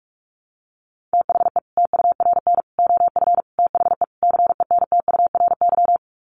Morse code, heard as a keyed tone, about 44 wpm, 700 Hz. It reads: THE TURN OF THE CENTURY
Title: Sets of 5 Words 45wpm